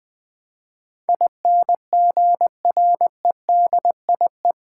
Morse code, heard as a keyed tone, about 20 wpm, 700 Hz. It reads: INGREDIE
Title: Build A Word - 4 Letters At A Time - 10-Letter Words - 20wpm